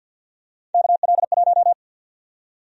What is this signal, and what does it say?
Morse code, heard as a keyed tone, about 52 wpm, 700 Hz. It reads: KL1
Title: USA Call Sign Prefixes - 50wpm